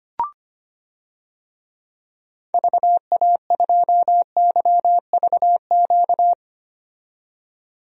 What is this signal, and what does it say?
Morse code, heard as a keyed tone, about 25 wpm, 700 Hz. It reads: VA2YVQ
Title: DXCC-100 Call Signs - 6 Characters - 25wpm